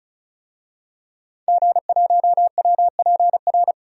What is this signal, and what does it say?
Morse code, heard as a keyed tone, about 34 wpm, 700 Hz. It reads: G1WPR
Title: DXCC-100 Call Signs - 5 Characters - 35wpm